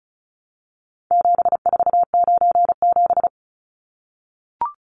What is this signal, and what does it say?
Morse code, heard as a keyed tone, about 34 wpm, 700 Hz. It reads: 7497
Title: Numbers - 4-Digit 35wpm